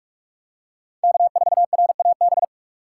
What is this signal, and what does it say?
Morse code, heard as a keyed tone, about 41 wpm, 700 Hz. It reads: K4RAB
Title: K1USN SST QSO Practice - 45wpm